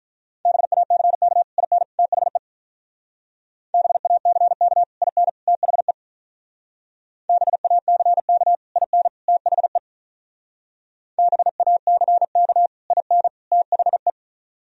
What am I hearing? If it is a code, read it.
Morse code, heard as a keyed tone, about 44 wpm, 700 Hz. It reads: BACK IN THE BACK IN THE BACK IN THE BACK IN THE